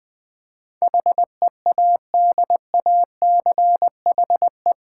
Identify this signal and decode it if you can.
Morse code, heard as a keyed tone, about 20 wpm, 700 Hz. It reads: HEADACHE
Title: Eight Letter Words - 20wpm